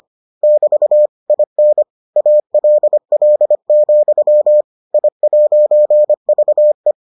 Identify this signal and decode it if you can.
Morse code, heard as a keyed tone, about 25 wpm, 600 Hz. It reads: <BT> IN ALL, I'VE